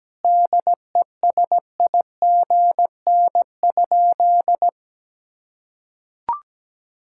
Morse code, heard as a keyed tone, about 17 wpm, 700 Hz. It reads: DESIGN?
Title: Sentences from Top 1000 Words - Review Part 5 - 17wpm